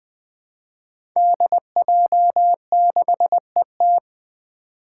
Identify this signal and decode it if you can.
Morse code, heard as a keyed tone, about 21 wpm, 700 Hz. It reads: DJ6ET
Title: Five Challenging Letter-Number Combinations 20wpm